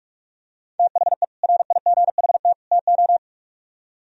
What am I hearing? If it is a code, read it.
Morse code, heard as a keyed tone, about 45 wpm, 700 Hz. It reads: THE RIGHT TO